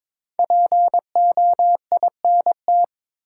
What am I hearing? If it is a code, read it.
Morse code, heard as a keyed tone, about 22 wpm, 700 Hz. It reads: POINT
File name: 5 Letter Words - First 3 Letters Given - 22wpm